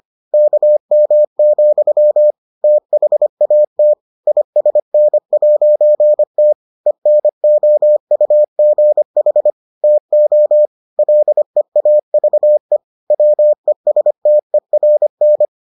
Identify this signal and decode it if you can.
Morse code, heard as a keyed tone, about 25 wpm, 600 Hz. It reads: KM, THAT ISN'T ENOUGH TO LEAVE WESTERN